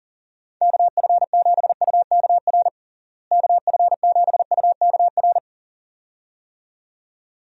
Morse code, heard as a keyed tone, about 40 wpm, 700 Hz. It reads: KF7UKR KF7UKR
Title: USA Call Signs - 6 Characters - Repeated 2x - 40wpm